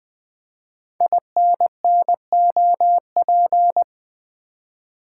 Morse code, heard as a keyed tone, about 20 wpm, 700 Hz. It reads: INNOP
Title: Code Groups - 5 Letters - 20wpm